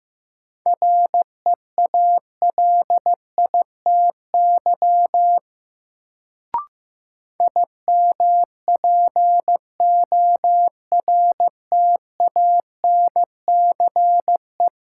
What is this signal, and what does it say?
Morse code, heard as a keyed tone, about 15 wpm, 700 Hz. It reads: REALITY IMPORTANCE
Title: Sentences from Top 1200 Words - New Word Drill Alt - 15wpm